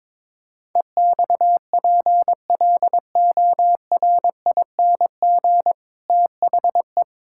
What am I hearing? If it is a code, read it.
Morse code, heard as a keyed tone, about 22 wpm, 700 Hz. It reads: EXPLORING THE
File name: Difficult Sentences with Hints - 22wpm